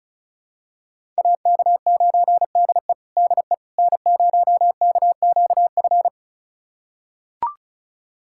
Morse code, heard as a keyed tone, about 36 wpm, 700 Hz. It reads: AK9DE DE N0KQF
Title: USA Callsign DE Callsign - 5 Characters - 35wpm